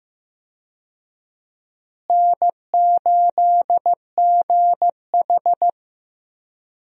Morse code, heard as a keyed tone, about 15 wpm, 700 Hz. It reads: N8GH
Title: USA Call Signs - 4 Characters - Repeated 3x - 15wpm